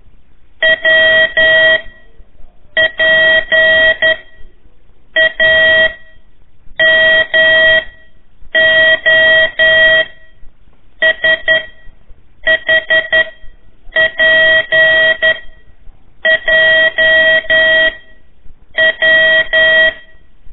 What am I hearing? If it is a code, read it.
Morse code, heard as a keyed tone, about 11 wpm, spaced as for 7 wpm, 1900 Hz. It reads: WPAMO SHPJW
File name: 3 lezione CW 1 AIR